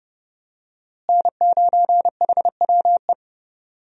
Morse code, heard as a keyed tone, about 30 wpm, 700 Hz. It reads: N9HWE
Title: USA Call Signs - 5 Characters - Repeated 2x - 30wpm